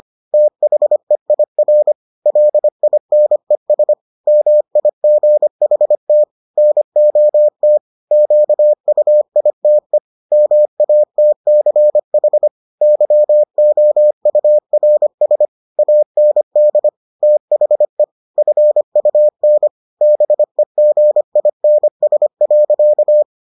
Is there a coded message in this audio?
Yes: Morse code from 0.3 to 23.2 s, heard as a keyed tone, about 25 wpm, 600 Hz. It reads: THEIR LINES MIGHT NOT QUITE MATCH YOURS AND THE FUN BEGINS.